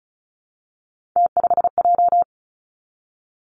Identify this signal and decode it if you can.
Morse code, heard as a keyed tone, about 36 wpm, 700 Hz. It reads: T5J
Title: Three Challenging Letter-Number Combinations 35wpm